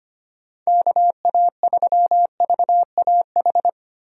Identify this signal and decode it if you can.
Morse code, heard as a keyed tone, about 25 wpm, 700 Hz. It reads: KA3VAH
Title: USA Call Signs - 6 Characters - Repeated 2x - 25wpm